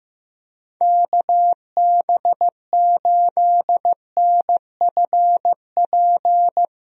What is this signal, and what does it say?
Morse code, heard as a keyed tone, about 15 wpm, 700 Hz. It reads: KB8NFP
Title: USA Call Signs - 6 Characters - Repeated 2x - 15wpm